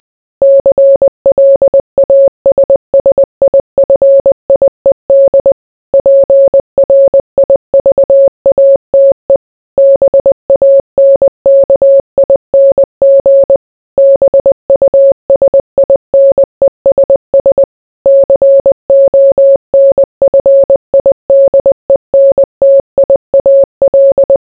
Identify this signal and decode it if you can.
Morse code, heard as a keyed tone, about 20 wpm, 550 Hz. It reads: CLASSIFIED PRIVATE BANKING BUSINESS CONFIDENTIAL